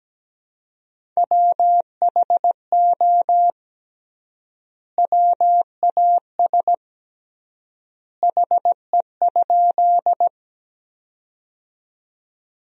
Morse code, heard as a keyed tone, about 17 wpm, 700 Hz. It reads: WHO WAS HE?
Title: 3x Word-Spacing - Sentences from Top 100 Words - 17wpm